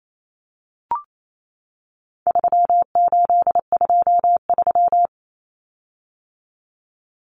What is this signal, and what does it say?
Morse code, heard as a keyed tone, about 28 wpm, 700 Hz. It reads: 3823
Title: Common Cut Numbers - 4-Digit 28wpm